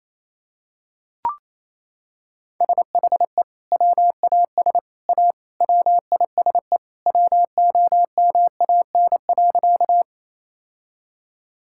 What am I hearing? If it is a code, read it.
Morse code, heard as a keyed tone, about 28 wpm, 700 Hz. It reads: SHE WAS A WISE WOMAN.
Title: Sentences from Top 1600 Words 28wpm